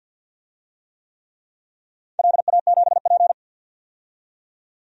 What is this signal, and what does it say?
Morse code, heard as a keyed tone, about 53 wpm, 700 Hz. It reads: LA7P
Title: DXCC-100 Call Signs - 4 Characters - 50wpm